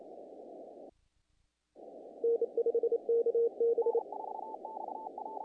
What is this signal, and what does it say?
Morse code, heard as a keyed tone, about 28 wpm, 460 Hz. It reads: N5KB
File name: contests / naqp-cw-k5tr-op-k5pi- -1800